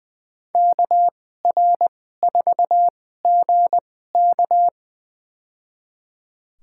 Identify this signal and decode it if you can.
Morse code, heard as a keyed tone, about 20 wpm, spaced as for 14 wpm, 700 Hz. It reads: KR4GK
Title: USA Call Signs - 2x Character Spacing - 20wpm